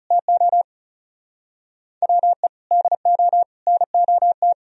Morse code, heard as a keyed tone, about 39 wpm, 700 Hz. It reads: TO WE DO NOT